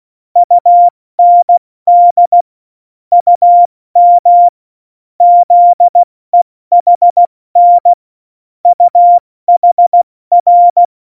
Morse code, heard as a keyed tone, about 16 wpm, 700 Hz. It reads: UND UM ZEHN UHR
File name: Morsekurs / Witze / Witz 29. Oktober 2019, 70 F80